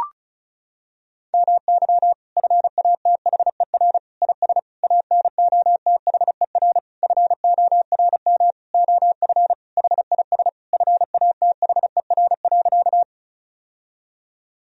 Morse code, heard as a keyed tone, about 36 wpm, 700 Hz. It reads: MY FATHER IS ANOTHER FORM OF HIS FATHER.
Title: Sentences from Top 1000 Words - Review Part 1 - 35wpm